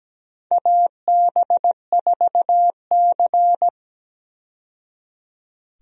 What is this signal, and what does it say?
Morse code, heard as a keyed tone, about 17 wpm, 700 Hz. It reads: AB4C
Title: USA Call Signs - 4 Characters - 17wpm